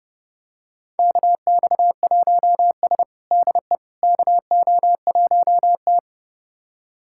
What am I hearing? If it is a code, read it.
Morse code, heard as a keyed tone, about 30 wpm, 700 Hz. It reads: KX1S DE KO1T
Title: USA Callsign DE Callsign - 4 Characters - 30wpm